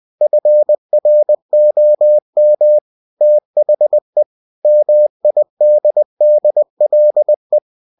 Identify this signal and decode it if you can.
Morse code, heard as a keyed tone, about 20 wpm, 600 Hz. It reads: FROM THE MIDDLE